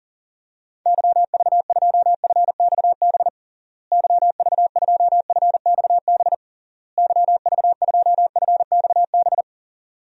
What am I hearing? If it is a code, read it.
Morse code, heard as a keyed tone, about 40 wpm, 700 Hz. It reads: YV2FXB YV2FXB YV2FXB
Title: DXCC-100 Call Signs - 6 Characters - Repeated 3x - 40wpm